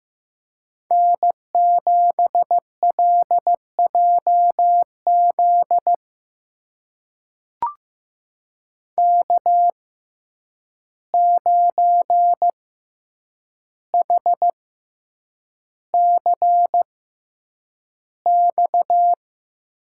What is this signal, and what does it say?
Morse code, heard as a keyed tone, about 15 wpm, spaced as for 8 wpm, 700 Hz. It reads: N7LJZ K9HCX
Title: USA Call Signs - 5 Characters - 6x Character Spacing - 15wpm